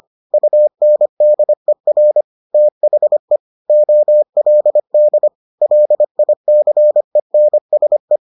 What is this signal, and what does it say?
Morse code, heard as a keyed tone, about 25 wpm, 600 Hz. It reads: UNDER THE OLD LICENSE